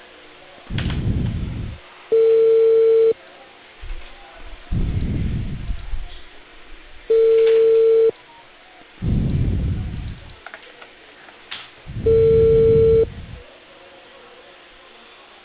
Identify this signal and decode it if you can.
Ringback tone (China, 450 Hz).